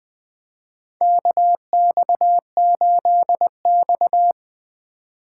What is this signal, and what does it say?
Morse code, heard as a keyed tone, about 20 wpm, 700 Hz. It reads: KX8X
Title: K1USN SST QSO Practice - 20wpm